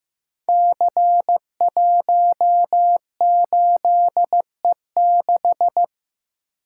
Morse code, heard as a keyed tone, about 15 wpm, 700 Hz. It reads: C18E6
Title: Code Groups - 5 Letter:Numbers - 15wpm